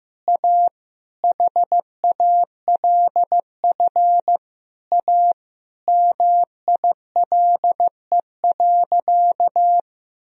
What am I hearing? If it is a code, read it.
Morse code, heard as a keyed tone, about 15 wpm, 700 Hz. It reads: A HALF A MILE.
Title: Sentences from Top 1000 Words - Review Part 1 - 15wpm